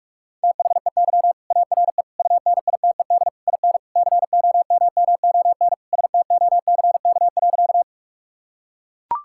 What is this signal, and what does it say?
Morse code, heard as a keyed tone, about 42 wpm, 700 Hz. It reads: THEY ARE UNITED IN COMMON STOCK.
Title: Sentences from Top 1000 Words - Review Part 5 - 45wpm